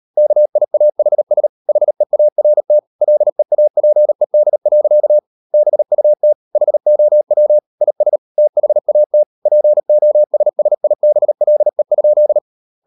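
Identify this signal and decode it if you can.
Morse code, heard as a keyed tone, about 38 wpm, 600 Hz. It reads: KIAHS HEART LEAPED. BUT HOW IS THAT POSSIBLE?
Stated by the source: Morse code practice at 38 WPM, sending Ken Follett's novel Never in English, from CW med Gnister.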